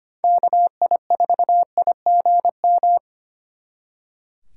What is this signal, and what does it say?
Morse code, heard as a keyed tone, about 33 wpm, 700 Hz. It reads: KI4IGM